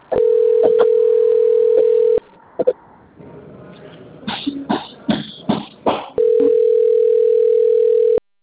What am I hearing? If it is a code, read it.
Ringback tone (North America), 440 + 480 Hz.